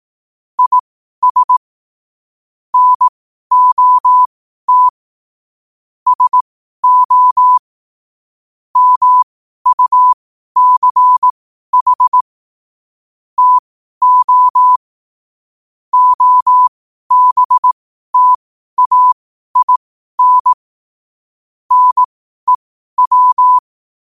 Morse code, heard as a keyed tone, about 18 wpm, spaced as for 12 wpm, 1000 Hz. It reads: IS NOT SO MUCH TO OBTAIN NEW
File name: qotdfiles / 12wpm